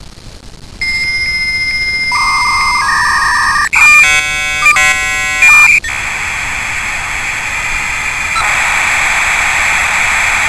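Dial-up modem handshake.
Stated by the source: Modem Sound ringtone.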